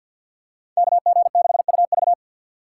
Morse code, heard as a keyed tone, about 52 wpm, 700 Hz. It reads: KG6UV
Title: USA Call Signs - 5 Characters - 6x Character Spacing - 50wpm